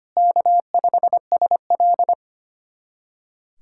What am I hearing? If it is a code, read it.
Morse code, heard as a keyed tone, about 25 wpm, 700 Hz. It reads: K5SL
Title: USA Callsign DE Callsign - 4 Characters - 25wpm